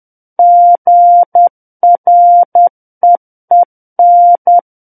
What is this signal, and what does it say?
Morse code, heard as a keyed tone, about 10 wpm, 700 Hz. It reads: GREEN
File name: Book One / 01-Chapter 1 - 10 WPM